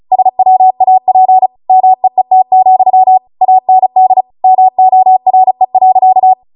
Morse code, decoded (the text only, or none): SWAP MEET, AND MORE.